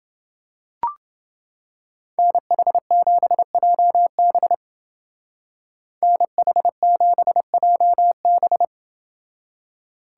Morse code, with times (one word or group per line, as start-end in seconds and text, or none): 2.2-4.5 NH7JB
6.0-8.6 NH7JB